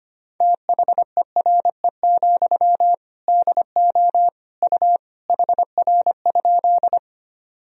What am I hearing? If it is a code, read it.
Morse code, heard as a keyed tone, about 25 wpm, 700 Hz. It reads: THERE, DO U HR?
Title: Sentences from Top 300 Words and Abbreviations 25wpm